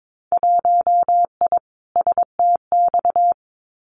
Morse code, heard as a keyed tone, about 22 wpm, 700 Hz. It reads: 1I STX
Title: Winter Field Day Exchanges 22wpm